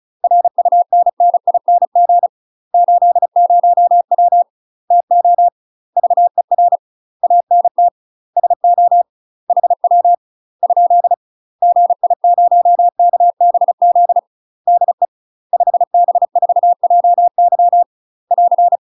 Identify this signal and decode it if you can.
Morse code, heard as a keyed tone, about 36 wpm, 700 Hz. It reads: RUNNING 80W TO VER ANT SO HW ? GI0KBZ DE 5B4JY <AR>
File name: UK MASTER QSOs TRAINING/35WPM